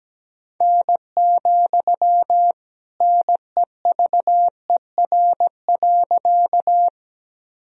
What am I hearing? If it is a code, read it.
Morse code, heard as a keyed tone, about 17 wpm, 700 Hz. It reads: N, NEVER.
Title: Sentences from Top 300 Words and Abbreviations 17wpm